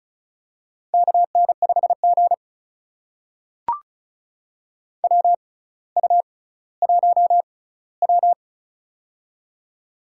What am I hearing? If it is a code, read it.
Morse code, heard as a keyed tone, about 36 wpm, spaced as for 18 wpm, 700 Hz. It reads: KN5G WU1W
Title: USA Call Signs - 4 Characters - 6x Character Spacing - 35wpm